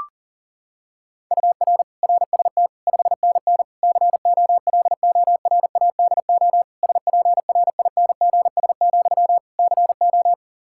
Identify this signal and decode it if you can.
Morse code, heard as a keyed tone, about 40 wpm, 700 Hz. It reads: UR RST 5NN COLORADO SPRINGS, CO
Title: Practice QSO - Exchange 1 - Beginning - 40wpm